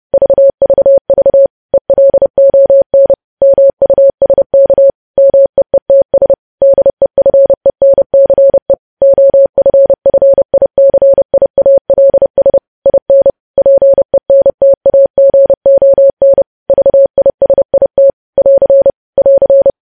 Morse code, heard as a keyed tone, about 30 wpm, 550 Hz. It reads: VVV ELON MUSK MEETS DEFENCE OFFICIALS IN PENTAGON VISIT <AR> <AR>
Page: news headlines in morse code at 30 wpm